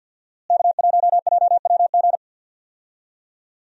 Morse code, heard as a keyed tone, about 47 wpm, 700 Hz. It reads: K1JWG
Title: USA Call Signs - 5 Characters - Repeated 3x - 50wpm